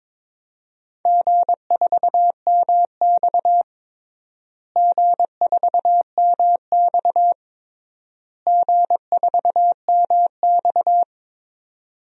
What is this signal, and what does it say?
Morse code, heard as a keyed tone, about 22 wpm, 700 Hz. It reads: G4MX G4MX G4MX
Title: DXCC-100 Call Signs - 4 Characters - Repeated 3x - 22wpm